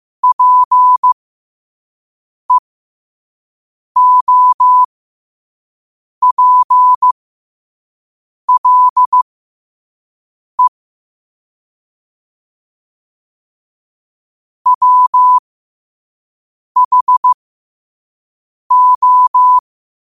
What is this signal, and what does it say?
Morse code, heard as a keyed tone, about 15 wpm, spaced as for 5 wpm, 1000 Hz. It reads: PEOPLE WHO
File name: qotdfiles / 5wpm